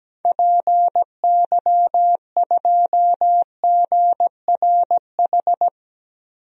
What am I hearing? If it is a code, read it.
Morse code, heard as a keyed tone, about 17 wpm, 700 Hz. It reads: PY2GRH
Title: DXCC-100 Call Signs - 6 Characters - Repeated 2x - 17wpm